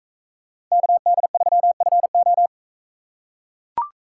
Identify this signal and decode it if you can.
Morse code, heard as a keyed tone, about 42 wpm, 700 Hz. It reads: KD3FO